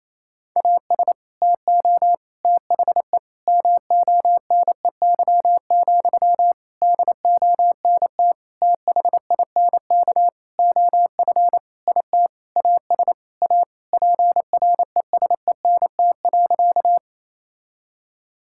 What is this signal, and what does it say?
Morse code, heard as a keyed tone, about 28 wpm, 700 Hz. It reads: AS TO THE MONEY, DONT THINK OF IT AS A PRESENT.
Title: Sentences from Top 1000 Words - Review Part 5 - 28wpm